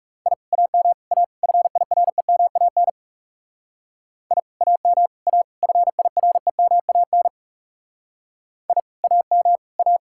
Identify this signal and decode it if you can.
Morse code, heard as a keyed tone, about 38 wpm, 700 Hz. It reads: I AM A FIREMAN I AM A FIREMAN I AM A